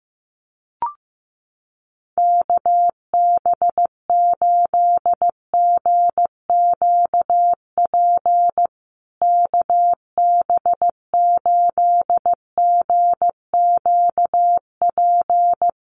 Morse code, heard as a keyed tone, about 15 wpm, 700 Hz. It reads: KB8GQP KB8GQP
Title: USA Call Signs Repeated 3x 15wpm